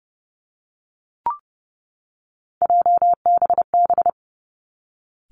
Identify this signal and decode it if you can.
Morse code, heard as a keyed tone, about 30 wpm, 700 Hz. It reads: JBB
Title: Three Challenging Letter-Number Combinations 30wpm